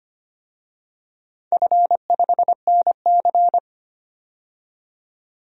Morse code, heard as a keyed tone, about 25 wpm, 700 Hz. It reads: F5NC
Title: DXCC-100 Call Signs - 4 Characters - 25wpm